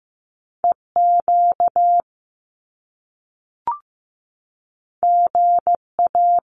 Morse code, heard as a keyed tone, about 15 wpm, 700 Hz. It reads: EQ GA
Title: Top Two Letter Combinations in Words 15wpm